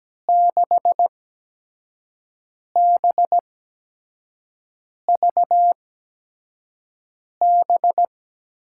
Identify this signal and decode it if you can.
Morse code, heard as a keyed tone, about 17 wpm, spaced as for 5 wpm, 700 Hz. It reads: 6BVB
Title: USA Call Signs - 6 Characters - 8x Character Spacing - 17wpm